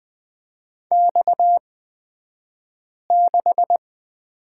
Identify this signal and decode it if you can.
Morse code, heard as a keyed tone, about 20 wpm, spaced as for 5 wpm, 690 Hz. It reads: X6
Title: 228 Lesson 37d - Letter x - Two Characters 20wpm